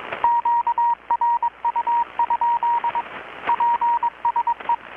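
Morse code, heard as a keyed tone, about 22 wpm, 940 Hz, with radio noise. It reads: QRU? PSE
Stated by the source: Morse from KPH's Night of Nights XX maritime radiotelegraphy event, received in Chiapas, Mexico about 3,800 km away on a Kenwood R-600 with a grounded 10 m long-wire antenna.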